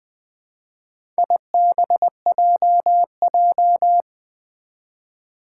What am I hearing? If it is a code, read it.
Morse code, heard as a keyed tone, about 20 wpm, 700 Hz. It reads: IBJJ
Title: Four Challenging Letter-Number Combinations 20wpm